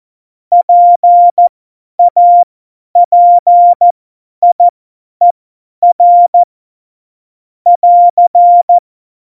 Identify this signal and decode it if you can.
Morse code, heard as a keyed tone, about 14 wpm, spaced as for 10 wpm, 700 Hz. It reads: PAPIER <AR>
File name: Morsekurs / Witze / Witz 28. November 2017, 50 F70